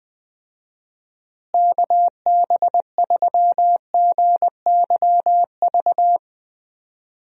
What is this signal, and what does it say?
Morse code, heard as a keyed tone, about 20 wpm, 700 Hz. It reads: KB3GYV
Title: USA Call Signs - 6 Characters - Repeated 2x - 20wpm